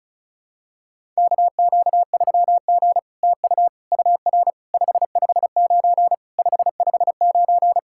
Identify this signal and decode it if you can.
Morse code, heard as a keyed tone, about 36 wpm, 700 Hz. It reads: KQ3G TU UR 559 559
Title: SOTA QSO Practice 35wpm